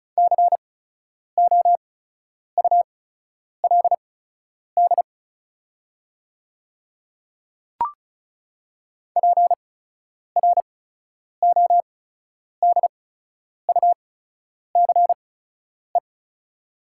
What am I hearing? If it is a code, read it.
Morse code, heard as a keyed tone, about 34 wpm, spaced as for 10 wpm, 690 Hz. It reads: COULD PRODUCE
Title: 8x Character Spacing - Top 500 Words - 35wpm